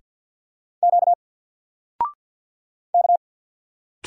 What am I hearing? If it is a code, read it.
Morse code, heard as a keyed tone, about 50 wpm, 700 Hz. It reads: Q K
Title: Single Character (Letter:Number:Punctuation:Prosign) 50wpm